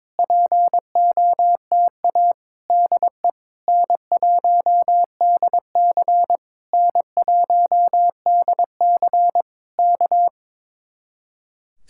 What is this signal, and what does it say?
Morse code, heard as a keyed tone, about 22 wpm, 700 Hz. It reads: POTA DE N1DC N1DC K
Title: Speed-Racing Common QSO Phrases - 22wpm